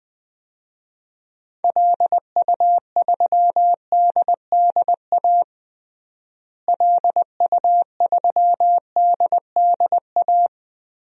Morse code, heard as a keyed tone, about 21 wpm, 700 Hz. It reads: LU3DDA LU3DDA
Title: DXCC-100 Call Signs - 6 Characters - Repeated 2x - 20wpm